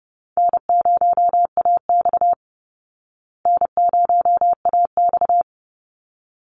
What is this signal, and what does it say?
Morse code, heard as a keyed tone, about 30 wpm, 700 Hz. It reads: N0AX N0AX
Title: Call Signs Repeated 3x - 30wpm